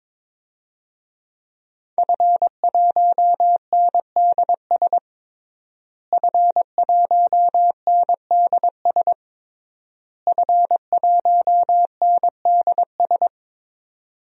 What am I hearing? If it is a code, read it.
Morse code, heard as a keyed tone, about 22 wpm, 700 Hz. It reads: F1NDS F1NDS F1NDS
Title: DXCC-100 Call Signs - 5 Characters - Repeated 3x - 22wpm